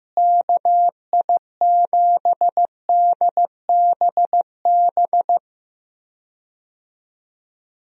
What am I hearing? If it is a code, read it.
Morse code, heard as a keyed tone, about 15 wpm, 700 Hz. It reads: KI7DBB
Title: USA Call Signs - 6 Characters - Repeated 2x - 15wpm